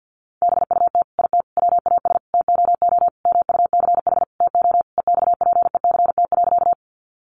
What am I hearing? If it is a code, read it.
Morse code, heard as a keyed tone, about 50 wpm, 700 Hz. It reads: BUT IT WAS TOO MUCH TO EXPECT.
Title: Sentences from Top 1100 Words 50wpm